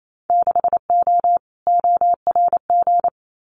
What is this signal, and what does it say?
Morse code, heard as a keyed tone, about 28 wpm, 700 Hz. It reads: 6O ORG
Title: Winter Field Day Exchanges Encore 28wpm